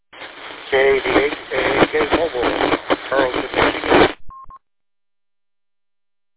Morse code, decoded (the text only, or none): none